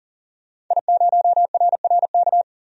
Morse code, heard as a keyed tone, about 40 wpm, 700 Hz. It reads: I0RRK
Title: DXCC-100 Call Signs - 5 Characters - 40wpm